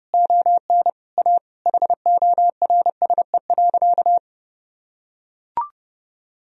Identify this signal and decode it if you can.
Morse code, heard as a keyed tone, about 30 wpm, 700 Hz. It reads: ON A HORSE.
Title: Sentences from Top 1000 Words - Review Part 1 - 30wpm